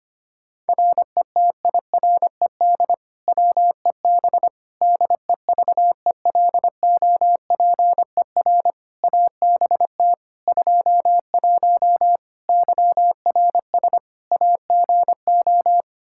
Morse code, heard as a keyed tone, about 25 wpm, 700 Hz. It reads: RETIRED WEB DEVELOPER ABT 21 YRS AGO
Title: Practice QSO - Exchange 3 - More Info - 25wpm